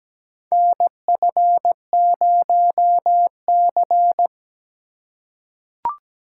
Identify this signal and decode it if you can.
Morse code, heard as a keyed tone, about 17 wpm, 700 Hz. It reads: NF0C
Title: USA Call Signs - 4 Characters - 4x Character Spacing - 17wpm